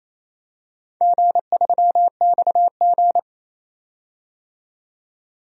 Morse code, heard as a keyed tone, about 27 wpm, 700 Hz. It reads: G3XG
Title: DXCC-100 Call Signs - 4 Characters - 28wpm